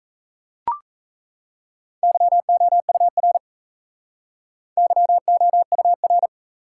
Morse code, heard as a keyed tone, about 34 wpm, 700 Hz. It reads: YOUR YOUR
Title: Speed-Racing Top 180 QSO Elements 28wpm